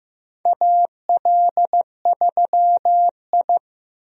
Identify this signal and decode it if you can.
Morse code, heard as a keyed tone, about 15 wpm, 700 Hz. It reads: AL3I
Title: USA Call Signs - 4 Characters - 15wpm